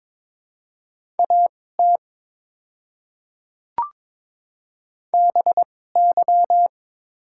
Morse code, heard as a keyed tone, about 22 wpm, spaced as for 16 wpm, 700 Hz. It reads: AT BY
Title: 2x Character Spacing - Top 100 Words - 22wpm